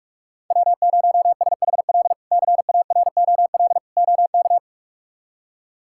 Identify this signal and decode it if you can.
Morse code, heard as a keyed tone, about 46 wpm, 700 Hz. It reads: W0SHL CAROL OK
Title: K1USN SST QSO Practice - 45wpm